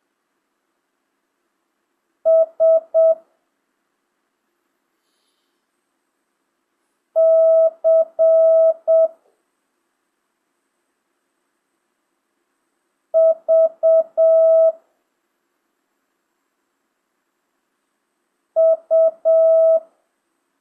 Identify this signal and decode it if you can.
Morse code, heard as a keyed tone, about 7 wpm, 650 Hz. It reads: S C V U
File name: jh8cbh / 2025cw